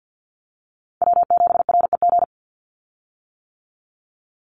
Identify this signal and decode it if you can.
Morse code, heard as a keyed tone, about 50 wpm, 700 Hz. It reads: W7REG
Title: USA Call Signs 50wpm